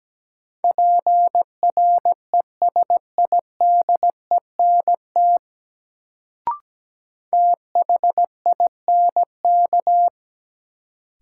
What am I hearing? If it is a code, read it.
Morse code, heard as a keyed tone, about 17 wpm, 700 Hz. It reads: PRESIDENT THINK
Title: Sets of 2 Words - New Word Drill - 17wpm